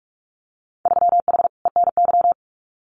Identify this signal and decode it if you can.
Morse code, heard as a keyed tone, about 46 wpm, 700 Hz. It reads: 3H ENY
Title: Winter Field Day Exchanges Encore 45wpm